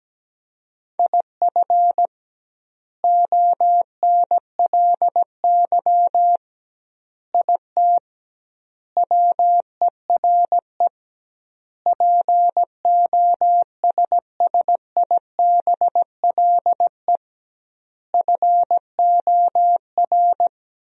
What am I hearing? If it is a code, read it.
Morse code, heard as a keyed tone, about 17 wpm, 700 Hz. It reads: IF ONLY IT WERE POSSIBLE FOR
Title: 2x Word-Spacing - Sentences from Top 500 Words - 17wpm